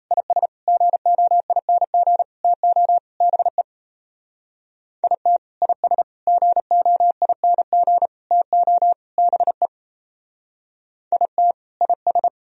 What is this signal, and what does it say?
Morse code, heard as a keyed tone, about 35 wpm, 700 Hz. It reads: IS GOING TO BE IT IS GOING TO BE IT IS
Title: Speed-Racing - Sets of 5 Words 28wpm